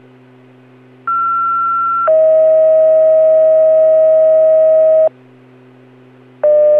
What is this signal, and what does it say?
Two-tone page, heard as a one-second tone, then a three-second tone, about 1360 then 640 Hz. It is heard from 1.1 to 5.1 s.